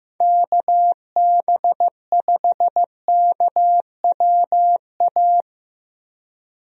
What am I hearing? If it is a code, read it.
Morse code, heard as a keyed tone, about 15 wpm, 700 Hz. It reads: KB5KWA
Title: USA Call Signs - 6 Characters - Repeated 3x - 15wpm